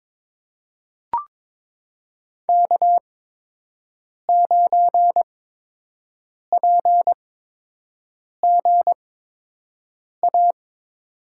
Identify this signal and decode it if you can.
Morse code, heard as a keyed tone, about 22 wpm, spaced as for 6 wpm, 700 Hz. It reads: K9PGA
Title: USA Call Signs - 5 Characters - 8x Character Spacing - 22wpm